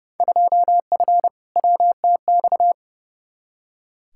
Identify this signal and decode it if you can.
Morse code, heard as a keyed tone, about 30 wpm, 700 Hz. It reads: 2F WTX
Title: Field Day Exchanges - 30wpm